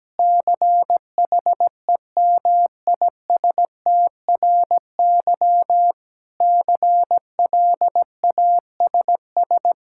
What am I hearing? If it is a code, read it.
Morse code, heard as a keyed tone, about 17 wpm, 700 Hz. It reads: CHEMISTRY CLASS
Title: Sentences from Top 1000 Words - Review Part 6 - 17wpm